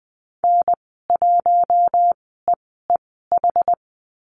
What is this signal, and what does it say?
Morse code, heard as a keyed tone, about 20 wpm, spaced as for 14 wpm, 700 Hz. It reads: N1EEH
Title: 082 Lesson 14g - Number 2 - Call Signs 20wpm